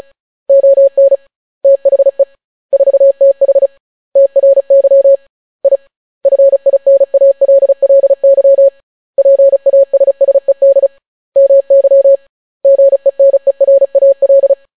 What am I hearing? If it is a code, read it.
Morse code, heard as a keyed tone, about 36 wpm, 550 Hz. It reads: ON THE 4TH TRY I FINALLY PASSED MY GENERAL